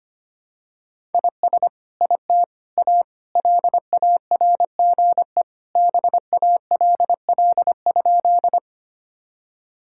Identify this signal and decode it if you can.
Morse code, heard as a keyed tone, about 25 wpm, 700 Hz. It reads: IS IT A LARGE BALL?
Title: Sentences from Top 1000 Words - Review Part 6 - 25wpm